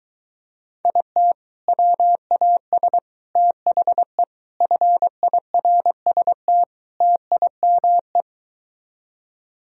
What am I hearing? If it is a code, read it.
Morse code, heard as a keyed tone, about 23 wpm, 700 Hz. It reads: IT WAS THE FIRST TIME